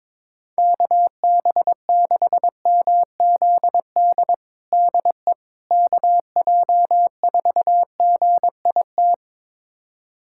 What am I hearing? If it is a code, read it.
Morse code, heard as a keyed tone, about 22 wpm, 700 Hz. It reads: KB6MZD DE KJ4GIT
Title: USA Callsign DE Callsign - 6 Characters - 22wpm